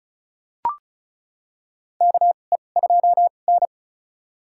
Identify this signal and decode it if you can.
Morse code, heard as a keyed tone, about 34 wpm, spaced as for 25 wpm, 700 Hz. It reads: KE2N
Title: USA Call Signs - 4 Characters - 2x Character Spacing - 35wpm